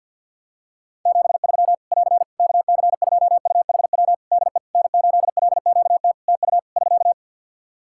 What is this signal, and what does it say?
Morse code, heard as a keyed tone, about 50 wpm, 700 Hz. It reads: 73 <AR> KC2UHW DE N8LQT TU <SK>
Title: Practice QSO - Exchange 4 - Ending - 50wpm